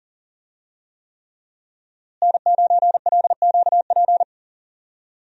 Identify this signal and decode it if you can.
Morse code, heard as a keyed tone, about 37 wpm, 700 Hz. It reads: N9LQP
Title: USA Call Signs - 5 Characters - Repeated 3x - 40wpm